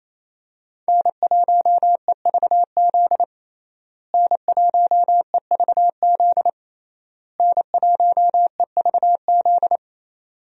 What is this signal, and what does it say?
Morse code, heard as a keyed tone, about 28 wpm, 700 Hz. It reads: N1EVZ N1EVZ N1EVZ
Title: USA Call Signs - 5 Characters - Repeated 3x - 28wpm